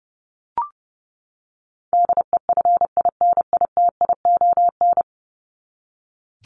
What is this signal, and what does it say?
Morse code, heard as a keyed tone, about 30 wpm, 700 Hz. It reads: DEFINITION
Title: Top 501-750 Words 30wpm